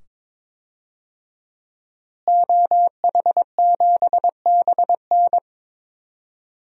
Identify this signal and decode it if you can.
Morse code, heard as a keyed tone, about 22 wpm, 700 Hz. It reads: OH7BN
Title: DXCC-100 Call Signs - 5 Characters - Repeated 2x - 22wpm